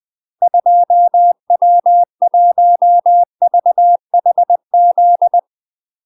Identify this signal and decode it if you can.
Morse code, heard as a keyed tone, about 20 wpm, 700 Hz. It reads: 2W1VHZ
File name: UK MASTER QSOs TRAINING / 20WPM / UK MASTER QSOs 0003